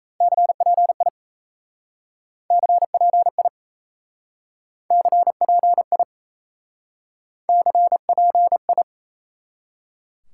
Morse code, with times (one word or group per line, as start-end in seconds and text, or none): 0.2-1.1 CPI
2.5-3.5 CPI
4.9-6.0 CPI
7.5-8.8 CPI